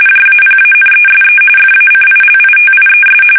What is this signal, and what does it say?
RTTY, about 45.45 baud. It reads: JAPANESE FACILITIES
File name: fsk850 / USS-Massachusetts